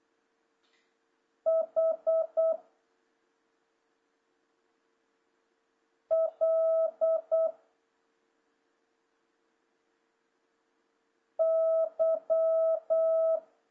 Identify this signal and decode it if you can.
Morse code, heard as a keyed tone, about 8 wpm, 650 Hz. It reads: H L Y
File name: jh8cbh / 2026cw